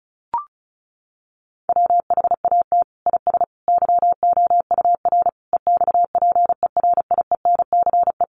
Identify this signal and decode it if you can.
Morse code, heard as a keyed tone, about 34 wpm, 700 Hz. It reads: WHAT IS YOUR EXPERIENCE
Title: Sentences from Top 600 Words 35wpm